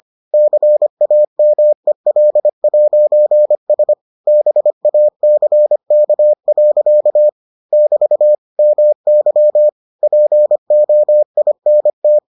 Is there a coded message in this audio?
Yes: Morse code from 0.3 to 12.2 s, heard as a keyed tone, about 25 wpm, 600 Hz. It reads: CAMEL'S BACK. <BT> MY POINT